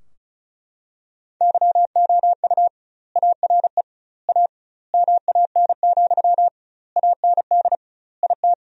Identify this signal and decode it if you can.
Morse code, heard as a keyed tone, about 36 wpm, 700 Hz. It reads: YOU ARE A MAN, AND IT
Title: 2x Word-Spacing - Sentences from Top 200 Words - 35wpm